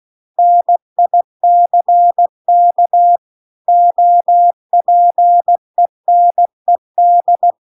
Morse code, heard as a keyed tone, about 16 wpm, 700 Hz. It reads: NICK OPENED